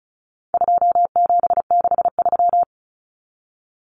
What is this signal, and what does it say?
Morse code, heard as a keyed tone, about 34 wpm, 700 Hz. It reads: 2763
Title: Common Cut Numbers - 4-Digit 35wpm